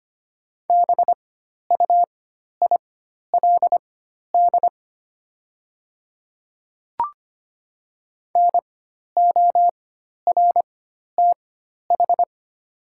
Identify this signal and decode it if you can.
Morse code, heard as a keyed tone, about 25 wpm, spaced as for 12 wpm, 690 Hz. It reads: BUILD NORTH
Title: 4x Character Spacing - Top 300 Words - 25wpm